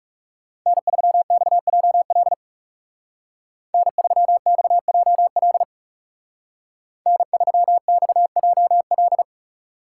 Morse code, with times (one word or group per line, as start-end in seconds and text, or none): 0.7-2.3 N3XJL
3.7-5.6 N3XJL
7.1-9.2 N3XJL